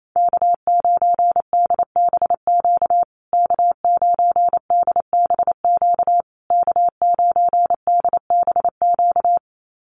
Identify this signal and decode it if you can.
Morse code, heard as a keyed tone, about 28 wpm, 700 Hz. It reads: K9DBQ K9DBQ K9DBQ
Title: USA Call Signs Repeated 3x 28wpm